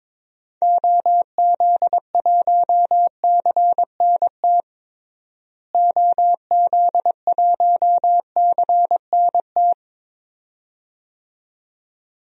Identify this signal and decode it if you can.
Morse code, heard as a keyed tone, about 22 wpm, 700 Hz. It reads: OZ1CNT OZ1CNT
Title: DXCC-100 Call Signs - 6 Characters - Repeated 2x - 22wpm